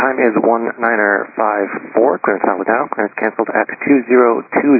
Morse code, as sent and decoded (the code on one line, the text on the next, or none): none